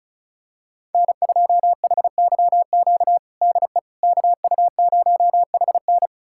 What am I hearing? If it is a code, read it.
Morse code, heard as a keyed tone, about 36 wpm, 700 Hz. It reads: N2HYQ DE KU0HN
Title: USA Callsign DE Callsign - 5 Characters - 35wpm